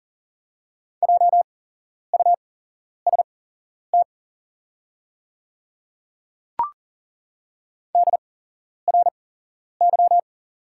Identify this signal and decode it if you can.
Morse code, heard as a keyed tone, about 40 wpm, spaced as for 11 wpm, 700 Hz. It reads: JUST DRY
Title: 8x Character Spacing - Top 500 Words - 40wpm